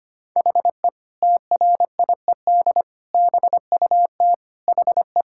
Morse code, heard as a keyed tone, about 25 wpm, 700 Hz. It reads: HE TRIED BUT HE
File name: practice / 25wpm / Sentences with 2-Word Phrases - 25wpm